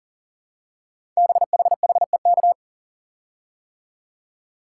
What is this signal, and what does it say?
Morse code, heard as a keyed tone, about 44 wpm, 700 Hz. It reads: BHHEK
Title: Code Groups - 5 Letters - 40wpm